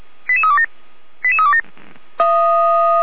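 Five-tone selective call (ZVEI), heard as quick runs of five tones, tones of about 70 ms. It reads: 88327 88327
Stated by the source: ZVEI five-tone selective call received via SDR.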